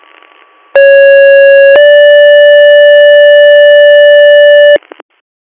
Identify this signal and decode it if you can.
Two-tone page, heard as a one-second tone, then a three-second tone, about 550 then 580 Hz, with noise. It is heard from 0.7 to 4.8 s.